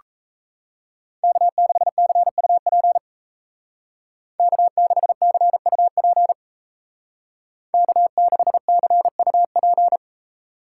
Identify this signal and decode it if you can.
Morse code, heard as a keyed tone, about 33 wpm, 700 Hz. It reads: K6CUP K6CUP K6CUP
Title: Speed Racing - USA 1x3 Call Signs 28wpm